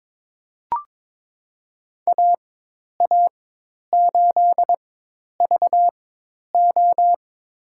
Morse code, heard as a keyed tone, about 22 wpm, spaced as for 10 wpm, 700 Hz. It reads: AA8VO
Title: USA Call Signs - 5 Characters - 4x Character Spacing - 22wpm